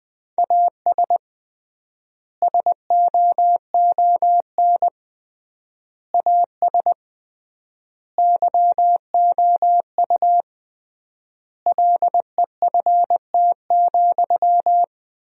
Morse code, heard as a keyed tone, about 20 wpm, 700 Hz. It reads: AS SOON AS YOU LEFT,
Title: 3x Word-Spacing - Sentences from Top 400 Words - 20wpm